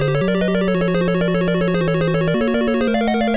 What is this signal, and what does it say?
There is no code.